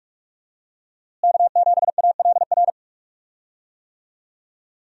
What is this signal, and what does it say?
Morse code, heard as a keyed tone, about 44 wpm, 700 Hz. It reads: K7ALR
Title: USA Call Signs - 5 Characters - Repeated 2x - 45wpm